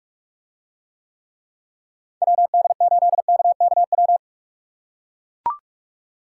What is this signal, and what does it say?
Morse code, heard as a keyed tone, about 44 wpm, 700 Hz. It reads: WD8KKW